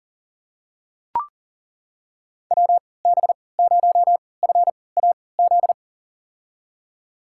Morse code, heard as a keyed tone, about 40 wpm, spaced as for 23 wpm, 700 Hz. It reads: WB0FAZ
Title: USA Call Signs - 3x Character Spacing - 40wpm